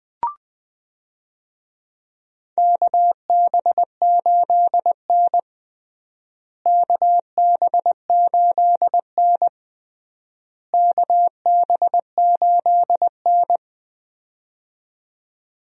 Morse code, heard as a keyed tone, about 20 wpm, 700 Hz. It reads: KB8N KB8N KB8N
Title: USA Call Signs - 4 Characters - Repeated 3x - 20wpm